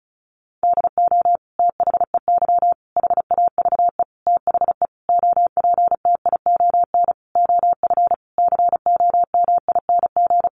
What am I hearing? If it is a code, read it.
Morse code, heard as a keyed tone, about 34 wpm, 700 Hz. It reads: DO THEY HAVE THE OPTION OF COMING
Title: Sentences from Top 1300 Words 35wpm